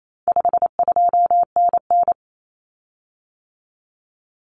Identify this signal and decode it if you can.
Morse code, heard as a keyed tone, about 28 wpm, 700 Hz. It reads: 52NN
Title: Common Cut Numbers - 4-Digit 28wpm